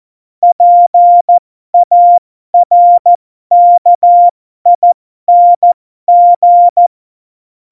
Morse code, heard as a keyed tone, about 14 wpm, spaced as for 12 wpm, 700 Hz. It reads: PARKING
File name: Morsekurs / Witze